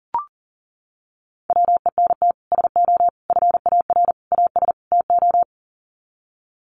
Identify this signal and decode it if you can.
Morse code, heard as a keyed tone, about 40 wpm, 700 Hz. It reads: WENT SO FAR AS TO
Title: Sets of 5 Words Encore 40wpm